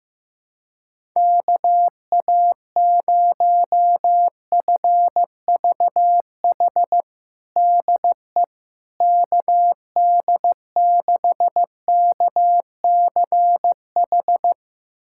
Morse code, heard as a keyed tone, about 15 wpm, 700 Hz. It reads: KA0FVH DE KD6KCH
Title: USA Callsign DE Callsign - 6 Characters - 15wpm